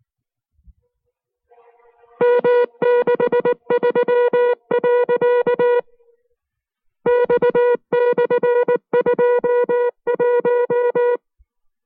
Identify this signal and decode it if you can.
Morse code, heard as a keyed tone, about 19 wpm, spaced as for 14 wpm, 480 Hz. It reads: M63.X /21
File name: FreeFiles / CW-practice / 15WPM-side1